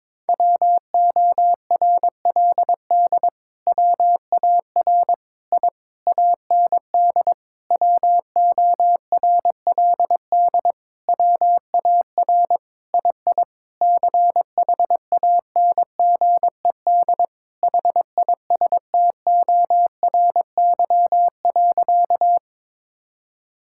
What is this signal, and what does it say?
Morse code, heard as a keyed tone, about 22 wpm, 700 Hz. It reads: WORLD WAR I AND WORLD WAR II CHANGED HISTORY.
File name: Sentences with 7-Word Phrases - 22wpm